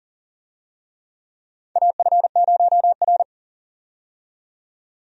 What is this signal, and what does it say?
Morse code, heard as a keyed tone, about 40 wpm, 700 Hz. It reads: AF0R
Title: USA Call Signs - 4 Characters - 40wpm